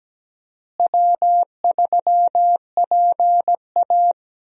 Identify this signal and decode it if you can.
Morse code, heard as a keyed tone, about 17 wpm, 700 Hz. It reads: W3PA
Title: USA Call Signs - 4 Characters - Repeated 3x - 17wpm